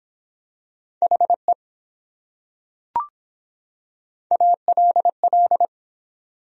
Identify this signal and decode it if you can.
Morse code, heard as a keyed tone, about 22 wpm, 700 Hz. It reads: HE ALL